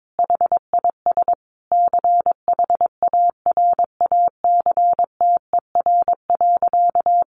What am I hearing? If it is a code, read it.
Morse code, heard as a keyed tone, about 22 wpm, 700 Hz. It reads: HIS CHARACTER.
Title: Sentences from Top 1200 Words 22wpm